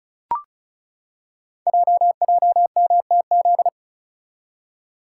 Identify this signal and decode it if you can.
Morse code, heard as a keyed tone, about 36 wpm, 690 Hz. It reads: JJMTZ
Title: Code Groups - 5 Letters - 35wpm